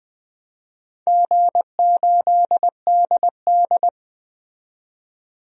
Morse code, heard as a keyed tone, about 20 wpm, 700 Hz. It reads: G8DD
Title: DXCC-100 Call Signs - 4 Characters - 20wpm